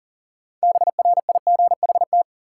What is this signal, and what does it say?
Morse code, heard as a keyed tone, about 40 wpm, 700 Hz. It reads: BRIGHT
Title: Six Letter Words - 40wpm